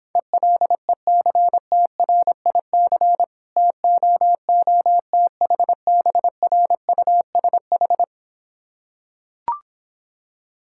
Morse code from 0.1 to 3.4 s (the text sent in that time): ELECTRIC